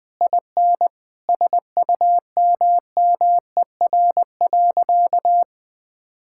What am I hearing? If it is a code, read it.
Morse code, heard as a keyed tone, about 20 wpm, 700 Hz. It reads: IN SUMMER.
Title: Sentences with 8-Word Phrases - 20wpm